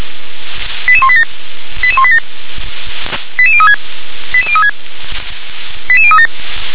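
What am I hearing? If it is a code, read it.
Five-tone selective call (ZVEI), heard as quick runs of five tones, tones of about 70 ms. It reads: 88187 88187 88036 88036 88037
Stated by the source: ZVEI five-tone selective call received on SDR.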